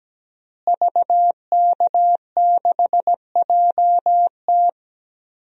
Morse code, heard as a keyed tone, about 17 wpm, 700 Hz. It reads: VK6JT
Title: DXCC-100 Call Signs - 5 Characters - Repeated 2x - 17wpm